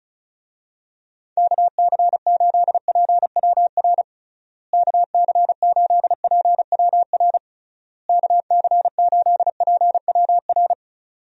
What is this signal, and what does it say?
Morse code, heard as a keyed tone, about 36 wpm, 700 Hz. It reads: KC8PWR KC8PWR KC8PWR
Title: USA Call Signs - 6 Characters - Repeated 3x - 35wpm